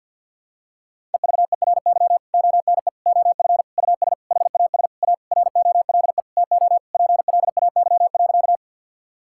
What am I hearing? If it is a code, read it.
Morse code, heard as a keyed tone, about 52 wpm, 700 Hz. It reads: EVERY ONE OF US HAS A ROLE TO PLAY.
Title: Sentences from Top 1000 Words - Review Part 6 - 50wpm